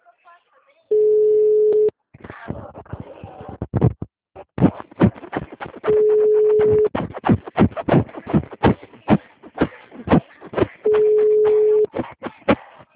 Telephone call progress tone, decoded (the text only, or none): ringback tone